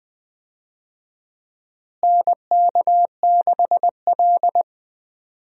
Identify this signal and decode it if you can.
Morse code, heard as a keyed tone, about 20 wpm, 700 Hz. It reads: NK6L
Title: USA Call Signs - 4 Characters - Repeated 2x - 20wpm